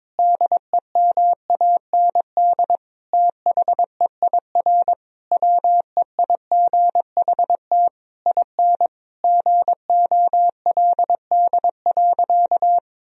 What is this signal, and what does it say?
Morse code, heard as a keyed tone, about 22 wpm, 700 Hz. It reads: DEMAND THEIR WEIGHT IN GOLD.
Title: Sentences from Top 1000 Words - Review Part 3 - 22wpm